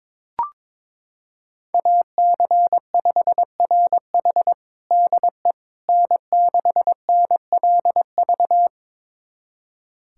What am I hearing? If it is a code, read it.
Morse code, heard as a keyed tone, about 22 wpm, 700 Hz. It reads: AC5RH DE N6NLV
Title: USA Callsign DE Callsign 22wpm